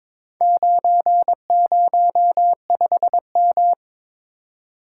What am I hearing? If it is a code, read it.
Morse code, heard as a keyed tone, about 22 wpm, 700 Hz. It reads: 905M
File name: practice / 22wpm / Four Letter-Number Combinations - 22wpm